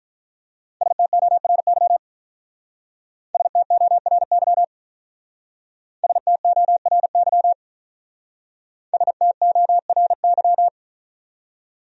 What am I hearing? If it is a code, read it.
Morse code, heard as a keyed tone, about 43 wpm, 700 Hz. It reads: STORY STORY STORY STORY
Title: Speed-Racing Top 500 Words 35wpm